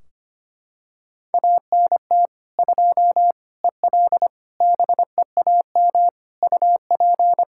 Morse code, heard as a keyed tone, about 25 wpm, 700 Hz. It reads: ANT 2 EL BEAM UP
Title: Practice QSO - Exchange 2 - Info - 25wpm